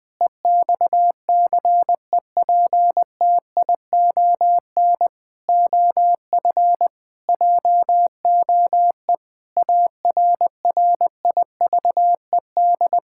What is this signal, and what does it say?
Morse code, heard as a keyed tone, about 20 wpm, 700 Hz. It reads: EXCEPTION OF JOE ARRIVED
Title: Sentences with 4-Word Phrases - Spoken Beginning - 20wpm